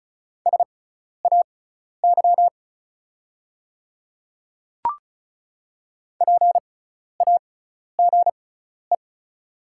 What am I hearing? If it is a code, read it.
Morse code, heard as a keyed tone, about 36 wpm, spaced as for 12 wpm, 690 Hz. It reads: SAY PAGE
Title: 6x Character Spacing - Top 500 Words - 35wpm